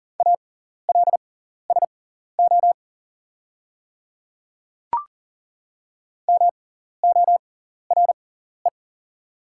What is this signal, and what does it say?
Morse code, heard as a keyed tone, about 40 wpm, spaced as for 14 wpm, 690 Hz. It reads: ALSO MORE
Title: 6x Character Spacing - Top 200 Words - 40wpm